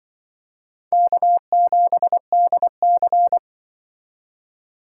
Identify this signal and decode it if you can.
Morse code, heard as a keyed tone, about 25 wpm, 700 Hz. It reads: K7DC
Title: Speed Racing - USA 1x2 Call Signs 20wpm